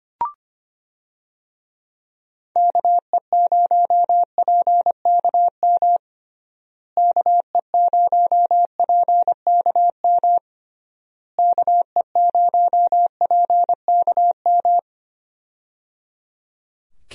Morse code, heard as a keyed tone, about 25 wpm, 700 Hz. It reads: KE0PKM KE0PKM KE0PKM
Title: USA Call Signs - 6 Characters - Repeated 3x - 25wpm